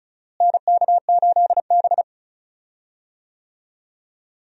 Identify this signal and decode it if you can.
Morse code, heard as a keyed tone, about 36 wpm, 700 Hz. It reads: NK8B
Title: USA Call Signs - 4 Characters - Repeated 2x - 35wpm